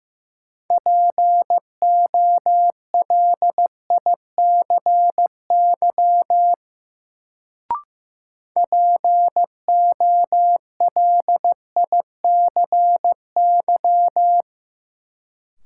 Morse code, heard as a keyed tone, about 15 wpm, 700 Hz. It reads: POLICY POLICY
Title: Sentences from Top 1000 Words - Review Part 5 - New Word Drill - 15wpm